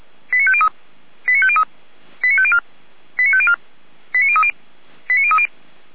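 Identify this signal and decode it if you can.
Five-tone selective call (ZVEI), heard as quick runs of five tones, tones of about 70 ms. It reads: 89503 89503 89504 89504 89030 89030
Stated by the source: ZVEI five-tone call received by SDR, archived by fire brigade Feuerwehr Orlamünde.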